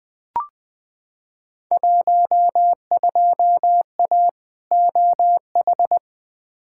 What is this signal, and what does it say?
Morse code, heard as a keyed tone, about 20 wpm, 700 Hz. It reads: 12A OH
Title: Field Day Exchanges Encore - 20wpm